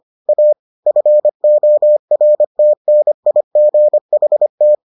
Morse code, heard as a keyed tone, about 25 wpm, 600 Hz. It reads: A FORTNIGHT